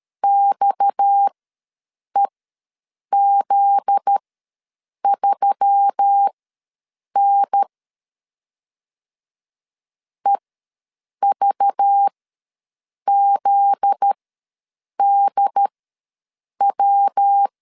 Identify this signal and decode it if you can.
Morse code, heard as a keyed tone, about 13 wpm, spaced as for 7 wpm, 800 Hz. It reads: XEZ3N EVZDW
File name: 7 GF9Z ws 12wpm